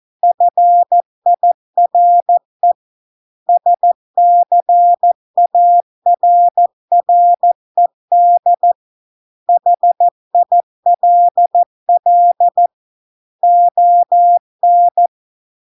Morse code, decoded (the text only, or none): FIRE SCARRED HILL ON